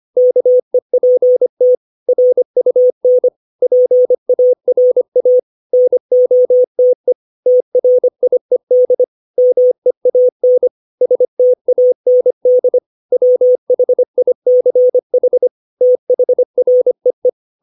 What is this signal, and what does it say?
Morse code, heard as a keyed tone, about 25 wpm, 500 Hz. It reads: KEPT RUN PARA NOTE TRIED MEAN STAND WHICH THREE
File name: 632 words / 632 Words 25WPM